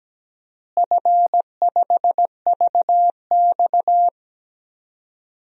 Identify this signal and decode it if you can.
Morse code, heard as a keyed tone, about 17 wpm, 700 Hz. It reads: F5VX
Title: DXCC-100 Call Signs - 4 Characters - Repeated 3x - 17wpm